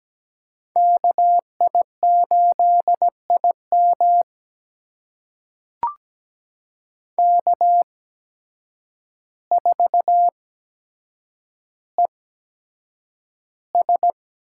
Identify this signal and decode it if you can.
Morse code, heard as a keyed tone, about 17 wpm, spaced as for 8 wpm, 700 Hz. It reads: KI8IM K4ES
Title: USA Call Signs - 5 Characters - 8x Character Spacing - 17wpm